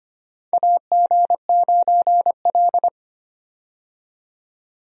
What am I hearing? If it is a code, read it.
Morse code, heard as a keyed tone, about 25 wpm, 700 Hz. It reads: AG9L
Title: USA Call Signs - 4 Characters - 25wpm